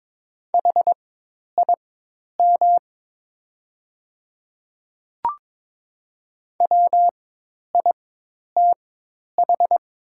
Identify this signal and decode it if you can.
Morse code, heard as a keyed tone, about 22 wpm, spaced as for 10 wpm, 700 Hz. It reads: HIM WITH
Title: 4x Character Spacing - Top 100 Words - 22wpm